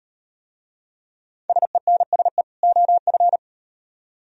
Morse code, heard as a keyed tone, about 38 wpm, 700 Hz. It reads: SENSE OF